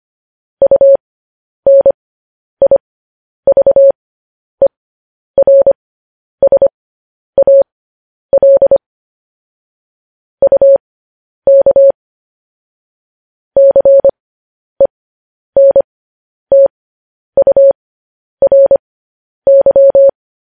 Morse code, heard as a keyed tone, about 25 wpm, spaced as for 10 wpm, 550 Hz. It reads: UNIVERSAL UK CENTURY
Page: news headlines in morse code at 10 wpm